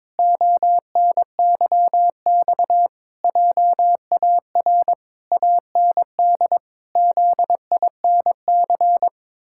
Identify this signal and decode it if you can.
Morse code, heard as a keyed tone, about 22 wpm, 700 Hz. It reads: ONYX JAR AND ZINC
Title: Pangram Challenge v2 - 22wpm